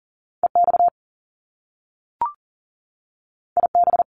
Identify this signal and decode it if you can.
Morse code, heard as a keyed tone, about 39 wpm, 700 Hz. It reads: EX IB
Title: Top Two Letter Combinations in Words 40wpm